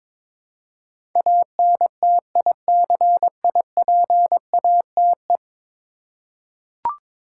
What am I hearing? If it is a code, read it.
Morse code, heard as a keyed tone, about 21 wpm, 700 Hz. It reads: ANTICIPATE
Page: morse-code-ninja-practice-22wpm